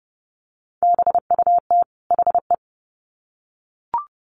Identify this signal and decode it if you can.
Morse code, heard as a keyed tone, about 30 wpm, 700 Hz. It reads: BUT HE
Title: Sets of 2 Words 30wpm